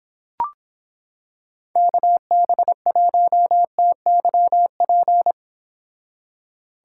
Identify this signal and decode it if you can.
Morse code, heard as a keyed tone, about 26 wpm, 700 Hz. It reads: KB1TYP